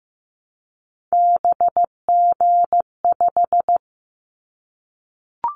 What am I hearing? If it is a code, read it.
Morse code, heard as a keyed tone, about 15 wpm, 700 Hz. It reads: BG5
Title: Callsign Prefixes for Top 100 DXCC - 15wpm